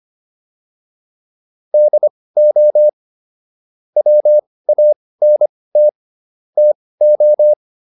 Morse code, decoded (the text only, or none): DO WANT TO